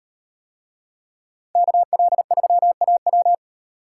Morse code, heard as a keyed tone, about 38 wpm, 700 Hz. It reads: KL3AW